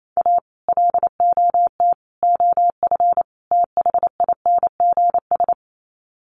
Morse code, heard as a keyed tone, about 28 wpm, 700 Hz. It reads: A LOT OF THINGS
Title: Sets of 4 Words 28wpm